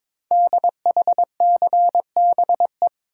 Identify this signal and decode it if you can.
Morse code, heard as a keyed tone, about 22 wpm, 700 Hz. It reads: DHCBE
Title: Code Groups - 5 Letters - 22wpm